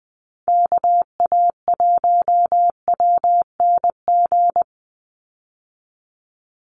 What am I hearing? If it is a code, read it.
Morse code, heard as a keyed tone, about 20 wpm, 700 Hz. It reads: KA1WNG
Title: USA Call Signs 20wpm